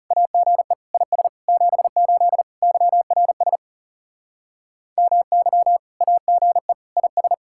Morse code, heard as a keyed tone, about 36 wpm, 700 Hz. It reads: AGE IS 78 YRS MY AGE IS
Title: Speed-Racing Common QSO Phrases - 30wpm